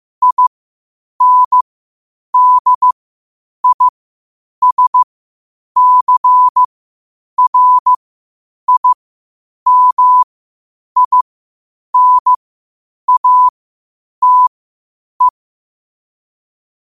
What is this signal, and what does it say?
Morse code, heard as a keyed tone, about 15 wpm, spaced as for 8 wpm, 1000 Hz. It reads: INDISCRIMINATE
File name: qotdfiles / 8wpm